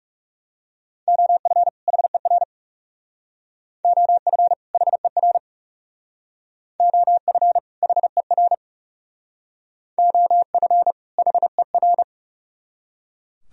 Morse code, heard as a keyed tone, about 37 wpm, 700 Hz. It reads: OF HER OF HER OF HER OF HER
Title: Speed-Racing - Sets of 2 Words 30wpm